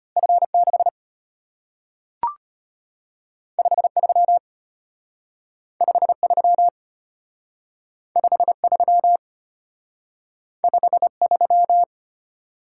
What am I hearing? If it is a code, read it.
Morse code, heard as a keyed tone, about 34 wpm, 700 Hz. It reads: F6 53 53 53 53